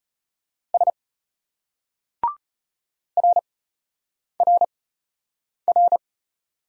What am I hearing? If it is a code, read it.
Morse code, heard as a keyed tone, about 34 wpm, spaced as for 8 wpm, 700 Hz. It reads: S RRR